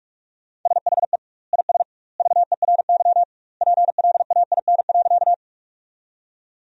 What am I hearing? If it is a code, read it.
Morse code, heard as a keyed tone, about 46 wpm, 700 Hz. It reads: SHE IS VERY PLAIN.
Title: 2x Word-Spacing - Sentences from Top 400 Words - 45wpm